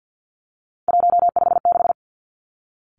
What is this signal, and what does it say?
Morse code, heard as a keyed tone, about 50 wpm, 700 Hz. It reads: 156
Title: Three Challenging Letter-Number Combinations 50wpm